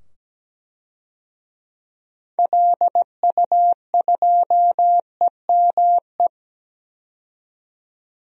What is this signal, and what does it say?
Morse code, heard as a keyed tone, about 17 wpm, 700 Hz. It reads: LU2EME
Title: DXCC-100 Call Signs - 6 Characters - 17wpm